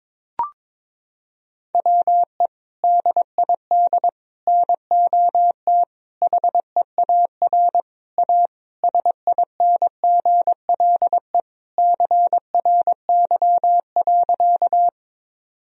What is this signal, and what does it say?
Morse code, heard as a keyed tone, about 22 wpm, 700 Hz. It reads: WE DID NOT HEAR A SINGLE CRY.
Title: Sentences from Top 1000 Words - Review Part 4 - 22wpm